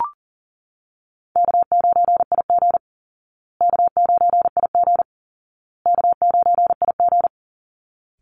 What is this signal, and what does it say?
Morse code, heard as a keyed tone, about 37 wpm, 700 Hz. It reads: K9IG K9IG K9IG
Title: Easy Call Signs Repeated 3x - 40wpm